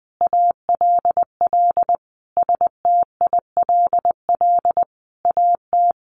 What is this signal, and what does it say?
Morse code, heard as a keyed tone, about 20 wpm, 700 Hz. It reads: ALL STILL AT
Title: Sentences from Top 1500 Words 20wpm